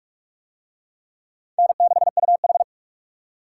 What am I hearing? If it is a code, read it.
Morse code, heard as a keyed tone, about 42 wpm, 700 Hz. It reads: N6UH